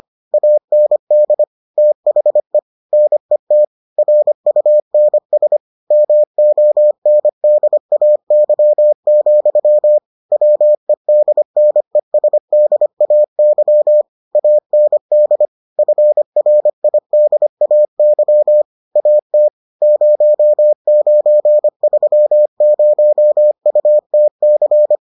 Morse code, heard as a keyed tone, about 25 wpm, 600 Hz. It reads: AND THE NET RUNS MONDAY, WEDNESDAY AND FRIDAY AT 0930UTC